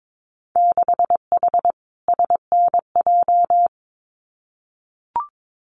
Morse code, heard as a keyed tone, about 22 wpm, 700 Hz. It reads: 6H SNJ
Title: Winter Field Day Exchanges Encore 22wpm